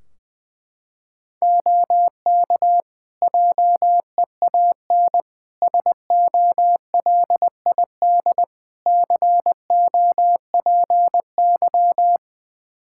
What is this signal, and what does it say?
Morse code, heard as a keyed tone, about 20 wpm, 700 Hz. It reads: OK JEAN SOLID COPY
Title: Practice QSO - Exchange 3 - More Info - 20wpm